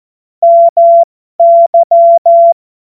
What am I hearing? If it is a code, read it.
Morse code, heard as a keyed tone, about 14 wpm, spaced as for 12 wpm, 680 Hz. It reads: MY
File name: Morsekurs / Witze